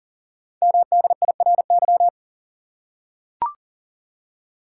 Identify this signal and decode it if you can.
Morse code, heard as a keyed tone, about 42 wpm, 690 Hz. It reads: MDIRY
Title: Code Groups - 5 Letters - 40wpm